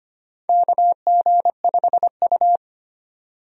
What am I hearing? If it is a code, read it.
Morse code, heard as a keyed tone, about 25 wpm, 700 Hz. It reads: KG5U
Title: USA Call Signs - 4 Characters - 25wpm